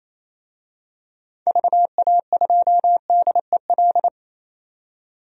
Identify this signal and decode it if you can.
Morse code, heard as a keyed tone, about 28 wpm, 700 Hz. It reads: VA2DEL
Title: DXCC-100 Call Signs - 6 Characters - 28wpm